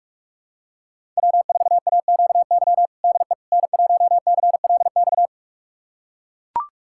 Morse code, heard as a keyed tone, about 41 wpm, 700 Hz. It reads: W4AQY DE N1CLX
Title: USA Callsign DE Callsign - 5 Characters - 45wpm